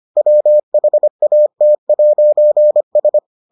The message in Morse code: WHAT'S

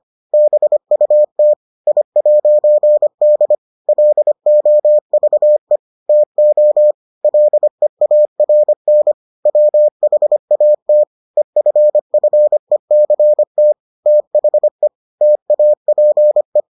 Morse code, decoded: BUT I'D LOVE TO LEARN WHAT EFFECT THE TAPE

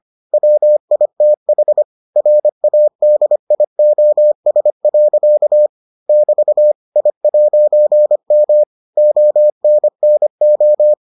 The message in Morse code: WITH RADIOS. <BT> I'M ONNO

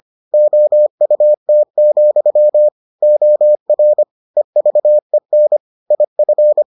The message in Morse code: OUT, OR EVEN IF

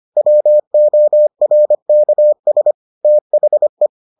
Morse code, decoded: WORKS THE

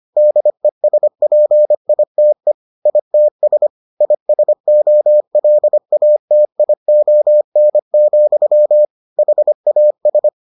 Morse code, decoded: DESPITE ITS ISOLATION, HAS

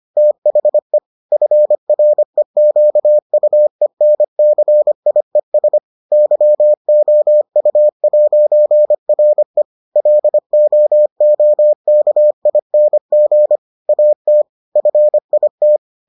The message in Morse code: THE FREQUENCIES YOU'RE LOOKING AT FIT

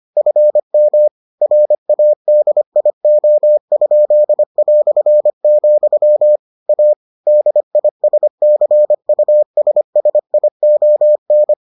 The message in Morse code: FM RADIO?", A DISCUSSION

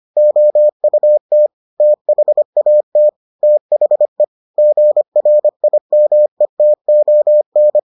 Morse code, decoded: OUT THAT THE GRIMETON

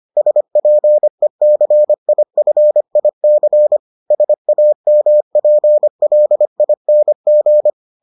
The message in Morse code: SPECIFIC SAMPLING

